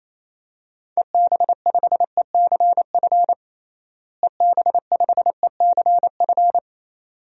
Morse code, heard as a keyed tone, about 27 wpm, 700 Hz. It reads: EB5ECF EB5ECF
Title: DXCC-100 Call Signs - 6 Characters - Repeated 2x - 28wpm